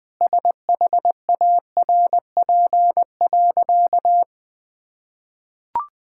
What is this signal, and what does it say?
Morse code, heard as a keyed tone, about 21 wpm, 700 Hz. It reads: SHARP.
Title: Sentences from Top 1600 Words 20wpm